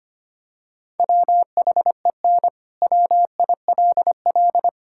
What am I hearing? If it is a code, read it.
Morse code, heard as a keyed tone, about 25 wpm, 700 Hz. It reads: WHEN WILL
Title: Sentences from Top 1000 Words - Review Part 6 - 25wpm